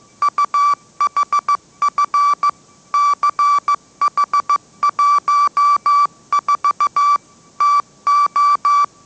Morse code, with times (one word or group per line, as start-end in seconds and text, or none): 0.2-2.5 UHF
2.9-7.2 CH14
7.6-8.8 TO